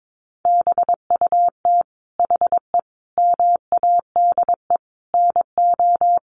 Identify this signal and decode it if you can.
Morse code, heard as a keyed tone, about 22 wpm, 700 Hz. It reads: BUT HE MADE NO
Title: Sentences from Top 1400 Words 22wpm